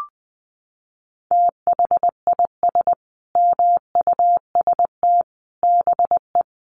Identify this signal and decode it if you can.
Morse code, heard as a keyed tone, about 20 wpm, 700 Hz. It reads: THIS MUST BE
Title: Sentences from Top 1100 Words 20wpm